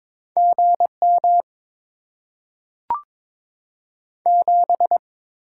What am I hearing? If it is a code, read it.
Morse code, heard as a keyed tone, about 23 wpm, 700 Hz. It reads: G M 7
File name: Top 20 QSO Elements - 8x Character Spacing - 22wpm